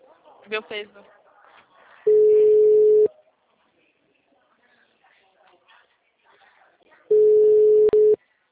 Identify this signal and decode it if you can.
Ringback tone (Europe, 425 Hz), with speech.
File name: out- -2004- - - .16251